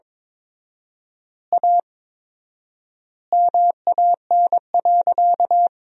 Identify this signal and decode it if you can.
Morse code, heard as a keyed tone, about 21 wpm, 700 Hz. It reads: A MAN.
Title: 4x Word-Spacing - Sentences from Top 300 Words - 22wpm